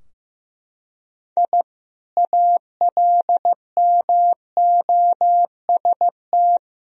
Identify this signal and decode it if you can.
Morse code, heard as a keyed tone, about 16 wpm, 700 Hz. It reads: I ALMOST
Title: Sentences from Top 1000 Words - Review Part 4 - 15wpm